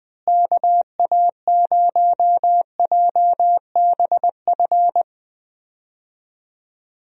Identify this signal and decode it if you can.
Morse code, heard as a keyed tone, about 20 wpm, 700 Hz. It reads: KA0JBF
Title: USA Call Signs - 6 Characters - Repeated 3x - 20wpm